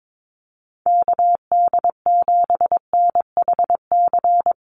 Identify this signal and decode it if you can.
Morse code, heard as a keyed tone, about 22 wpm, 700 Hz. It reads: KD7NHC
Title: USA Call Signs Repeated 3x 22wpm